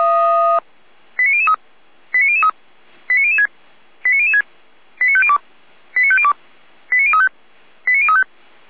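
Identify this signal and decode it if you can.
Five-tone selective call (ZVEI), heard as quick runs of five tones, tones of about 70 ms. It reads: 89003 89003 89006 89006 89502 89502 89035 89035
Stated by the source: ZVEI five-tone call 89160 received by SDR, from audio.f85360.feuerwehr-orlamuende.org fire brigade.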